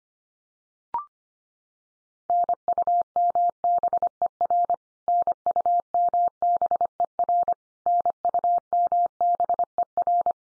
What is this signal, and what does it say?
Morse code, heard as a keyed tone, about 25 wpm, 700 Hz. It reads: NUMBER NUMBER NUMBER
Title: Top 100 Words 25wpm x3